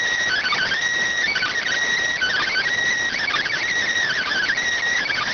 Emergency Alert System (SAME) tones (header bursts), heard as harsh buzzing bursts. It is heard from 0.2 to 4.6 s.